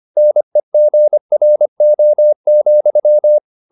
Morse code, heard as a keyed tone, about 25 wpm, 600 Hz. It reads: NEGRO,